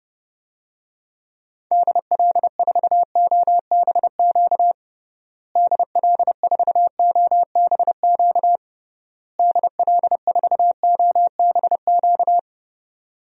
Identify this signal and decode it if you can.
Morse code, heard as a keyed tone, about 32 wpm, 700 Hz. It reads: DL4OBQ DL4OBQ DL4OBQ
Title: USA Call Signs - 6 Characters - Repeated 3x - 30wpm